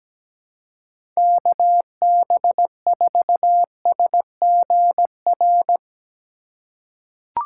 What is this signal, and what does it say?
Morse code, heard as a keyed tone, about 17 wpm, 700 Hz. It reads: KB4SGR
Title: USA Call Signs - 6 Characters - 4x Character Spacing - 17wpm